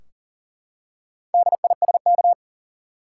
Morse code, heard as a keyed tone, about 40 wpm, 700 Hz. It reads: DISK
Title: Four Letter Words - 40wpm